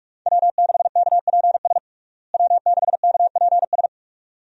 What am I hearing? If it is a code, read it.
Morse code, heard as a keyed tone, about 42 wpm, 700 Hz. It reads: W6KPS W6KPS
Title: USA Call Signs - 5 Characters - Repeated 2x - 45wpm